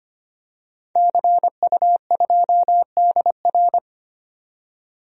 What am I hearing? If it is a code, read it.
Morse code, heard as a keyed tone, about 25 wpm, 700 Hz. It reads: CU2DR
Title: DXCC-100 Call Signs - 5 Characters - 25wpm